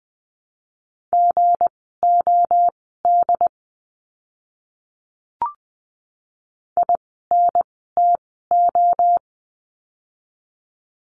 Morse code, heard as a keyed tone, about 20 wpm, spaced as for 15 wpm, 700 Hz. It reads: GOD INTO
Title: 160 Lesson 26f - Letter g - Words 20wpm